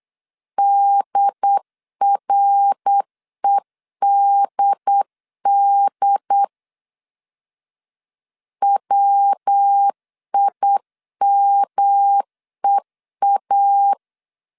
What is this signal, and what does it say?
Morse code, heard as a keyed tone, about 8 wpm, 800 Hz. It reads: DREDD WIMEA